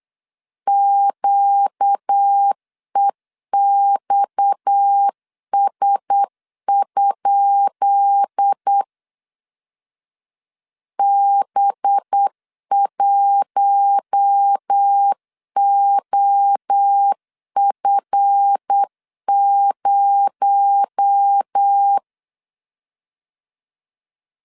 Morse code, heard as a keyed tone, about 8 wpm, 800 Hz. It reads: QEXS? B1OF0
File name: LEZIONI / 10 SIGNS / 10B INT 8wpm 3534